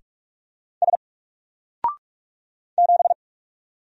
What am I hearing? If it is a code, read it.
Morse code, heard as a keyed tone, about 44 wpm, 700 Hz. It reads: S 7
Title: Single Character (Letter:Number:Punctuation:Prosign) 45wpm